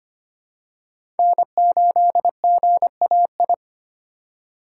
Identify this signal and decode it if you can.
Morse code, heard as a keyed tone, about 25 wpm, 700 Hz. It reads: N8GAI
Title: USA Call Signs - 5 Characters - 4x Character Spacing - 25wpm